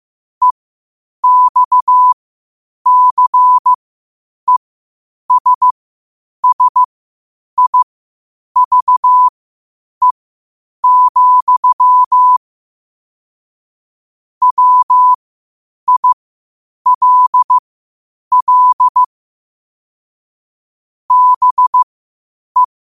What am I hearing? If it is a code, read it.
Morse code, heard as a keyed tone, about 15 wpm, spaced as for 8 wpm, 1000 Hz. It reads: EXCESSIVE, WILL BE
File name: 8wpm